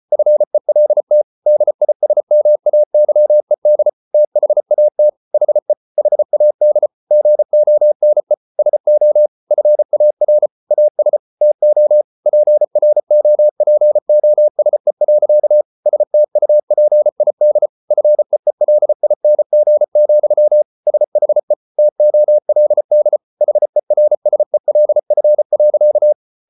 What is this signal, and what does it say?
Morse code, heard as a keyed tone, about 34 wpm, 600 Hz. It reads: FELT DISMAYED THAT HE HAD GONE SO FAR AS TO PROPOSE. STUPID FEELING, SHE TOLD HERSELF.